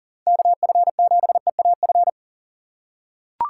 Morse code, heard as a keyed tone, about 40 wpm, 700 Hz. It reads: KF7EAF
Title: USA Call Signs - 6 Characters - 2x Character Spacing - 40wpm